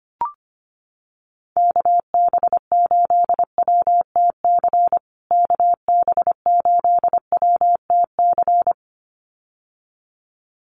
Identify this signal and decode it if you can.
Morse code, heard as a keyed tone, about 25 wpm, 700 Hz. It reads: KB8WTC KB8WTC
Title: USA Call Signs Repeated 2x 25wpm